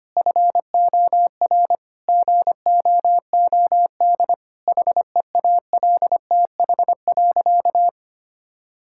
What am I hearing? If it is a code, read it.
Morse code, heard as a keyed tone, about 25 wpm, 700 Hz. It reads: FOR GOOD HEALTH.
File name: Easy Sentences with Hints - 25wpm